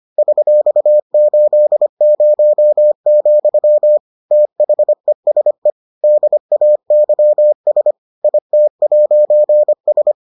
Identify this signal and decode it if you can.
Morse code, heard as a keyed tone, about 25 wpm, 600 Hz. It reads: $80, THESE DAYS IT'S